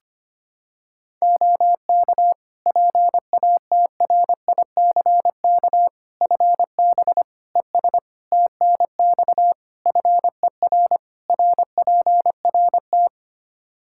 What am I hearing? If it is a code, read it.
Morse code, heard as a keyed tone, about 25 wpm, 700 Hz. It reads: OK PATRICK FB ES TNX FER RPRT
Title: Practice QSO - Exchange 2 - Info - 25wpm